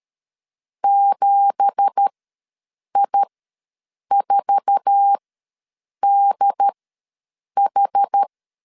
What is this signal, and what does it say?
Morse code, heard as a keyed tone, about 13 wpm, spaced as for 7 wpm, 800 Hz. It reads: 7I4DH
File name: LEZIONI / 9 2867 ws 12wpm 3431